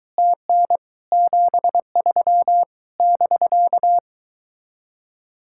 Morse code, heard as a keyed tone, about 22 wpm, 700 Hz. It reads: TN 73 <BK>